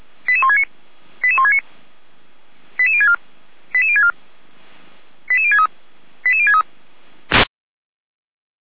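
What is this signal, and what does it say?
Five-tone selective call (ZVEI), heard as quick runs of five tones, tones of about 70 ms. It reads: 88169 88169 88064 88064 88063 88063
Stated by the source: ZVEI five-tone selective call received by SDR.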